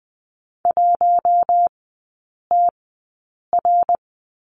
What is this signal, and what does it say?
Morse code, heard as a keyed tone, about 20 wpm, spaced as for 8 wpm, 700 Hz. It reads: 1TR
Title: 107 Lesson 18e - Letter m - Three Characters 20wpm